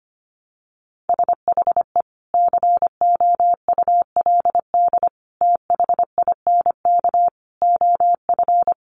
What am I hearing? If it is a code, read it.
Morse code, heard as a keyed tone, about 25 wpm, 700 Hz. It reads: SHE COULD THINK OF
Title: Sentences from Top 500 Words 25wpm